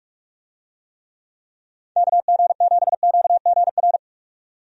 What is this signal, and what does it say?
Morse code, heard as a keyed tone, about 44 wpm, 700 Hz. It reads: KG7QGR
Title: USA Call Signs - 6 Characters - 45wpm